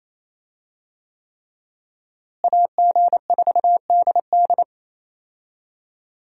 Morse code, heard as a keyed tone, about 27 wpm, 700 Hz. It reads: AG4DD
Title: USA Call Signs - 5 Characters - Repeated 3x - 28wpm